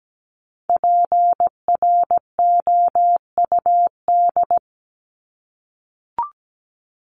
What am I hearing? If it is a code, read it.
Morse code, heard as a keyed tone, about 17 wpm, 700 Hz. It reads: PROUD
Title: Top 1501-1750 Words 17wpm